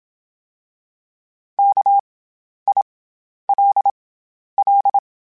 Morse code, heard as a keyed tone, about 27 wpm, spaced as for 11 wpm, 800 Hz. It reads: KILL
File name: Special INGs-mp3s / Special INGs-mp3s / SlowToFaster ing @25